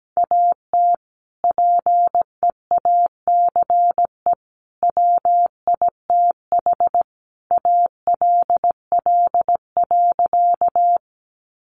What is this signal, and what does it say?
Morse code, heard as a keyed tone, about 17 wpm, 700 Hz. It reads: AT PEACE WITH ALL.
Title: Sentences from Top 1500 Words 17wpm